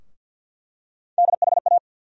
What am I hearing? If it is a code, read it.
Morse code, heard as a keyed tone, about 51 wpm, 700 Hz. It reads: DHA
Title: USA Call Sign Postfixes - 3 Letters 50wpm